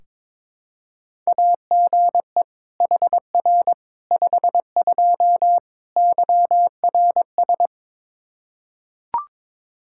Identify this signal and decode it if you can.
Morse code, heard as a keyed tone, about 22 wpm, 700 Hz. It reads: AGE HR 52 YRS
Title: Practice QSO - Exchange 3 - More Info - 22wpm